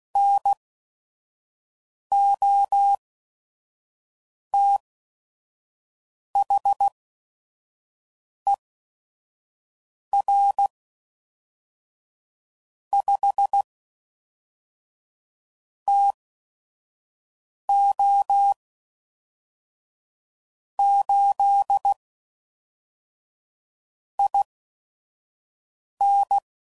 Morse code, heard as a keyed tone, about 16 wpm, spaced as for 5 wpm, 800 Hz. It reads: NOTHER 5 TO 8 IN
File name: CW Code Groups @5WPM / QSOs / QSO WA7QOX 1